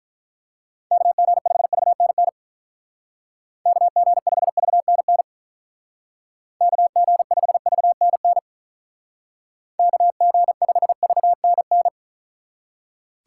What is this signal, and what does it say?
Morse code, heard as a keyed tone, about 43 wpm, 700 Hz. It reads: KG5VNN KG5VNN KG5VNN KG5VNN
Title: Speed Racing - USA 2x3 Call Signs 35wpm